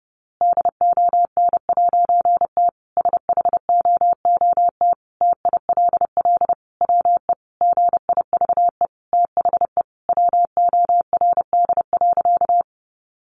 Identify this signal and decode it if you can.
Morse code, heard as a keyed tone, about 28 wpm, 700 Hz. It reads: DON'T SHOOT TILL WE GIVE THE WORD.
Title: A Sound Of Thunder 30wpm